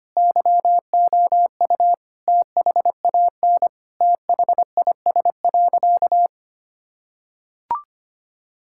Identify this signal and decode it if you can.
Morse code, heard as a keyed tone, about 25 wpm, 700 Hz. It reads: YOU THAN THIS.
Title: Sentences from Top 1000 Words - Review Part 6 - 25wpm